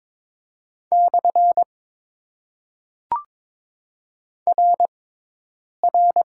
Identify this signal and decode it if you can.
Morse code, heard as a keyed tone, about 22 wpm, 700 Hz. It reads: / R R
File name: Top 90 QSO Elements - 6x Character Spacing - 22wpm